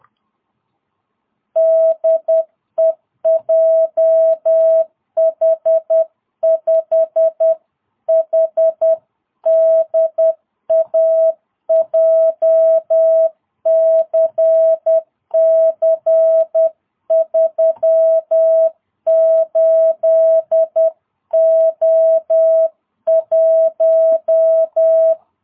Morse code, heard as a keyed tone, about 10 wpm, 650 Hz. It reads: DE JH5HDA JCC 38O1
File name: jh8cbh / 2022cw